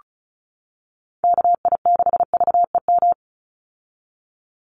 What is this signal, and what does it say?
Morse code, heard as a keyed tone, about 34 wpm, 700 Hz. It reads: KI6VEM
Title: USA Call Signs 35wpm